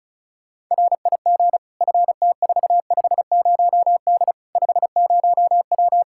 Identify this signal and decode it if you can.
Morse code, heard as a keyed tone, about 36 wpm, 700 Hz. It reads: RIG FT450D 50W
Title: Practice QSO - Exchange 2 - Info - 35wpm